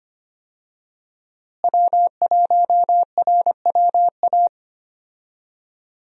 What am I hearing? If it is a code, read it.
Morse code, heard as a keyed tone, about 25 wpm, 700 Hz. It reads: W1RWA
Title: USA Call Signs - 5 Characters - Repeated 2x - 25wpm